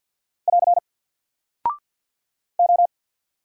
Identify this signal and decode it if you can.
Morse code, heard as a keyed tone, about 48 wpm, 700 Hz. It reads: <AR> O
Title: Single Character (Letter:Number:Punctuation:Prosign) Alt 50wpm